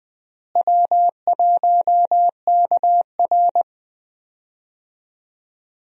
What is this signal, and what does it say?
Morse code, heard as a keyed tone, about 20 wpm, 700 Hz. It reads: W1KR
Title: USA Call Signs - 4 Characters - Repeated 3x - 20wpm